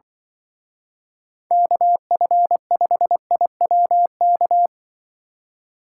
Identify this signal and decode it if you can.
Morse code, heard as a keyed tone, about 24 wpm, 700 Hz. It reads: KF5IWK
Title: Speed Racing - USA 2x3 Call Signs 20wpm